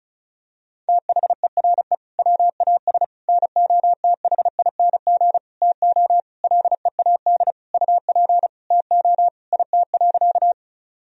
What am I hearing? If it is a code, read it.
Morse code, heard as a keyed tone, about 34 wpm, 700 Hz. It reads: THERE WAS NOTHING TO LEAD UP TO IT.
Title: Sentences from Top 1000 Words - Review Part 2 - 35wpm